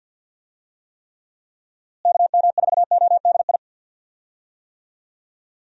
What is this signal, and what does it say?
Morse code, heard as a keyed tone, about 45 wpm, 700 Hz. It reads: KM4ODI
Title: USA Call Signs - 6 Characters - Repeated 2x - 50wpm